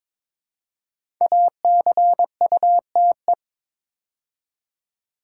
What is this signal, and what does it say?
Morse code, heard as a keyed tone, about 22 wpm, 700 Hz. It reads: ACUTE
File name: 5 Letter Words - First 3 Letters Given - 22wpm